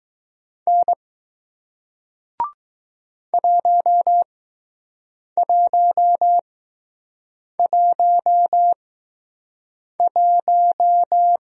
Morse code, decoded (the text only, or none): N 1111